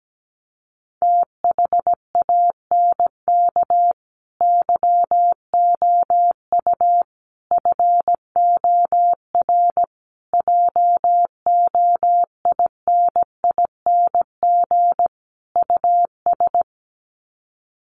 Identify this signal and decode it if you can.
Morse code, heard as a keyed tone, about 17 wpm, 700 Hz. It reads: THANK YOU FOR JOINING US
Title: Sets of 5 Words 17wpm